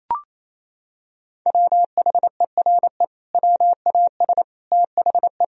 Morse code, heard as a keyed tone, about 28 wpm, 700 Hz. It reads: WHERE WAS THE
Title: Sentences from Top 1000 Words - Review Part 1 - 28wpm